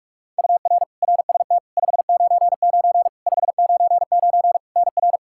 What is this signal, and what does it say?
Morse code, heard as a keyed tone, about 46 wpm, 700 Hz. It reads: UR RST 599 599 NR
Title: Practice QSO - Exchange 1 - Beginning - 45wpm